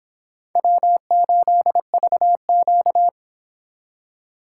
Morse code, heard as a keyed tone, about 24 wpm, 700 Hz. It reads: W8VQ